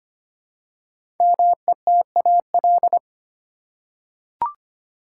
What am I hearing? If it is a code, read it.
Morse code, heard as a keyed tone, about 25 wpm, 700 Hz. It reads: METAL
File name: 5 Letter Words - First 3 Letters Given - 25wpm